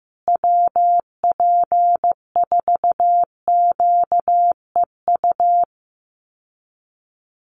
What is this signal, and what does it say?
Morse code, heard as a keyed tone, about 15 wpm, 700 Hz. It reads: WP4QEU
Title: USA Call Signs Repeated 3x 15wpm